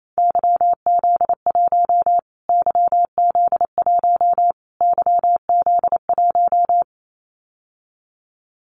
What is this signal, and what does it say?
Morse code, heard as a keyed tone, about 28 wpm, 700 Hz. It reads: YZ1 YZ1 YZ1
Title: Callsign Prefixes for Top 25 DXCC - Repeated 3x - 28wpm